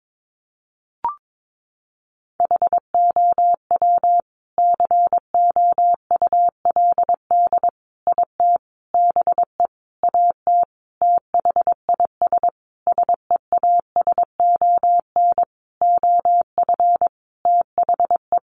Morse code, heard as a keyed tone, about 22 wpm, 700 Hz. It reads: HOW COULD IT BE AT THIS SEASON OF THE
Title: Sentences from Top 1400 Words 22wpm